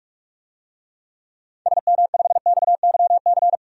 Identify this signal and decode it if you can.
Morse code, heard as a keyed tone, about 44 wpm, 690 Hz. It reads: SM5XYC
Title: DXCC-100 Call Signs - 6 Characters - 45wpm